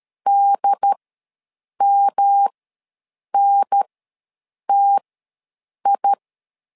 Morse code, heard as a keyed tone, about 13 wpm, spaced as for 7 wpm, 800 Hz. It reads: DMNTI
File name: LEZIONI / 3 MWRO / 3 M ws 12wpm 1745